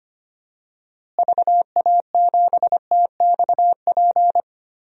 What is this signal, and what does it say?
Morse code, heard as a keyed tone, about 25 wpm, 700 Hz. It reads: VA7TXP
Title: DXCC-100 Call Signs - 6 Characters - Repeated 2x - 25wpm